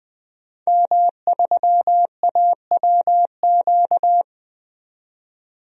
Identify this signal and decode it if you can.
Morse code, heard as a keyed tone, about 20 wpm, 700 Hz. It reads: M3AWQ
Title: DXCC-100 Call Signs - 5 Characters - 20wpm